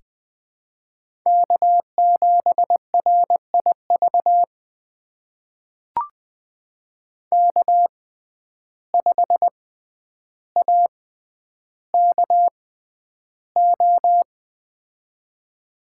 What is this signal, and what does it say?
Morse code, heard as a keyed tone, about 20 wpm, spaced as for 10 wpm, 700 Hz. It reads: K7RIV K5AKO
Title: USA Call Signs - 5 Characters - 6x Character Spacing - 20wpm